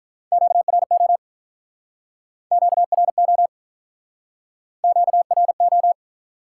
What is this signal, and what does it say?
Morse code, heard as a keyed tone, about 48 wpm, 700 Hz. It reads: QRO QRO QRO